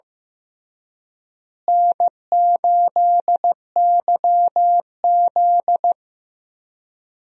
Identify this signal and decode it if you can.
Morse code, heard as a keyed tone, about 15 wpm, 700 Hz. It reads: N8YZ
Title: USA Call Signs - 4 Characters - Repeated 3x - 15wpm